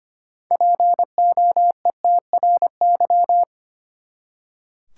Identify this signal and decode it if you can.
Morse code, heard as a keyed tone, about 25 wpm, 700 Hz. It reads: POETRY
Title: Six Letter Words - 25wpm